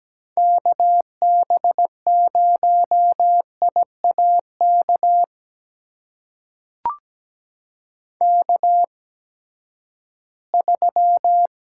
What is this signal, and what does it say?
Morse code, heard as a keyed tone, about 17 wpm, spaced as for 12 wpm, 700 Hz. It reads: KB0IAK K3
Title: USA Call Signs - 8x Character Spacing - 17wpm